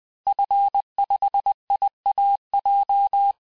Morse code, heard as a keyed tone, about 20 wpm, 800 Hz. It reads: F5IAJ
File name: cours-du-lundi files / cours-1584-Fin-de-QSO-vitesse-19mpm